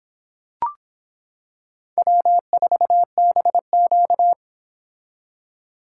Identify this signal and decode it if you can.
Morse code, heard as a keyed tone, about 26 wpm, 700 Hz. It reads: W4BQ